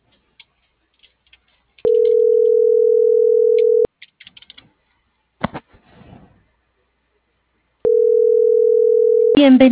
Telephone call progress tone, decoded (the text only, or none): ringback tone